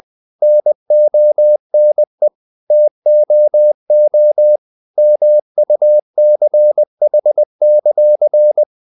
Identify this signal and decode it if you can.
Morse code, heard as a keyed tone, about 20 wpm, 600 Hz. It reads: NONE TOO MUCH;
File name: CW / Practice / If20